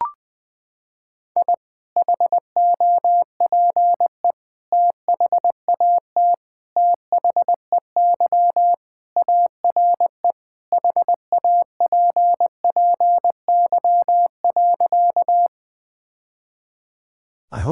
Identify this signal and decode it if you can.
Morse code, heard as a keyed tone, about 20 wpm, 700 Hz. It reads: I HOPE THAT THEY ARE HAPPY.
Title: Sentences with 3-Word Phrases - 20wpm